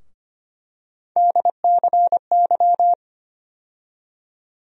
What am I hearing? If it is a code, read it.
Morse code, heard as a keyed tone, about 25 wpm, 700 Hz. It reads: DCY
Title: USA Call Sign Postfixes - 3 Letters 25wpm